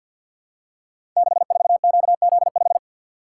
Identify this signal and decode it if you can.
Morse code, heard as a keyed tone, about 52 wpm, 700 Hz. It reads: 64QZ5
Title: Code Groups - 5 Letter:Numbers - 50wpm